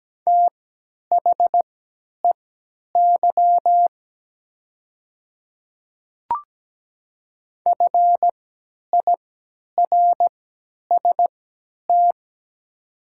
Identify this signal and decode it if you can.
Morse code, heard as a keyed tone, about 17 wpm, spaced as for 10 wpm, 700 Hz. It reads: THEY FIRST
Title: 3x Character Spacing - Top 500 Words - 17wpm